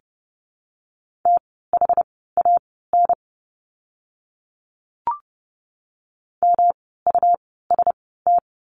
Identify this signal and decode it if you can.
Morse code, heard as a keyed tone, about 30 wpm, spaced as for 17 wpm, 700 Hz. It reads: THAN MUST
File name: course / 30:10wpm / 237 Lesson 38f - Letter q - Words 30wpm